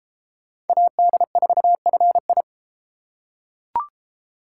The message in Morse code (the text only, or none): AD4FI